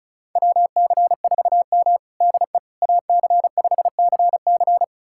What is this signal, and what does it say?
Morse code, heard as a keyed tone, about 36 wpm, 700 Hz. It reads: WC4M DE AC5CC
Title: USA Callsign DE Callsign 35wpm